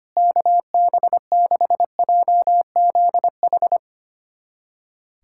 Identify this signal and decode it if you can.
Morse code, heard as a keyed tone, about 25 wpm, 700 Hz. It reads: KB6JZH
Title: USA Callsign DE Callsign - 6 Characters - 25wpm